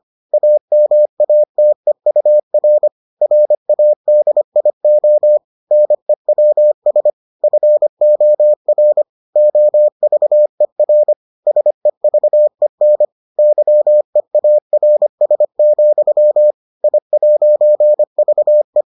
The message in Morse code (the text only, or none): AMATEUR RADIO NEWS FOR OVER SEVEN YEARS, I'VE